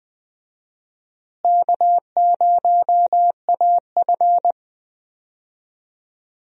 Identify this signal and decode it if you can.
Morse code, heard as a keyed tone, about 20 wpm, 700 Hz. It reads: K0AF
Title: USA Call Signs - 4 Characters - Repeated 2x - 20wpm